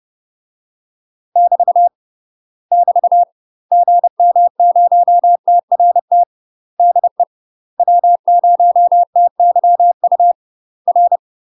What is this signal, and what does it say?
Morse code, heard as a keyed tone, about 30 wpm, 700 Hz. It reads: <BT> <BT> GM0TRT DE W0TYU R
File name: UK MASTER QSOs TRAINING / 30WPM / UK MASTER QSOs 0051